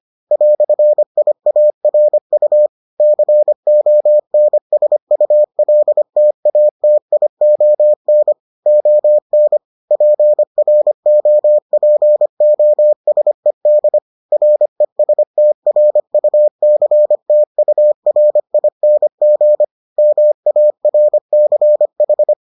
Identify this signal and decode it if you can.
Morse code, heard as a keyed tone, about 25 wpm, 600 Hz. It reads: "IARU CONSULTATION ON PROPOSED RESTRUCTURING MARCH